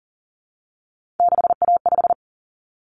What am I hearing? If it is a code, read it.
Morse code, heard as a keyed tone, about 40 wpm, 700 Hz. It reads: 6A5
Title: Common Cut Numbers - 3-Digit 40wpm